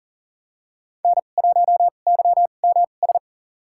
Morse code, heard as a keyed tone, about 37 wpm, spaced as for 28 wpm, 700 Hz. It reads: N1YMS
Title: USA Call Signs - 2x Character Spacing - 40wpm